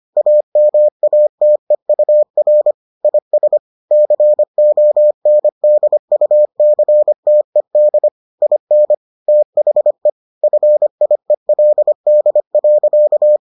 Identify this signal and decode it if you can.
Morse code, heard as a keyed tone, about 25 wpm, 600 Hz. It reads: AMATEUR IS CONDUCTED IN THE FIELD.